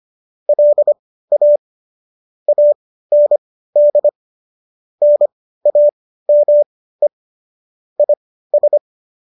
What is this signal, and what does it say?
Morse code, heard as a keyed tone, about 25 wpm, spaced as for 15 wpm, 590 Hz. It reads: LA AND NAME IS